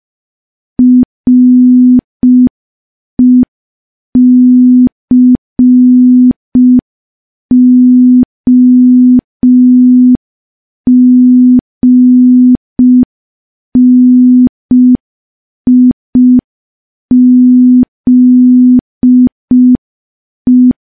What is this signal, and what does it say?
Morse code, heard as a keyed tone, about 5 wpm, 250 Hz. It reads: RECOGNIZE
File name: morse / morse-training-05WPM